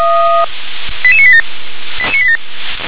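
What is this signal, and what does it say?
Five-tone selective call (ZVEI), heard as quick runs of five tones, tones of about 70 ms. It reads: 88987 88987
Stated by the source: ZVEI five-tone selective call received with an SDR.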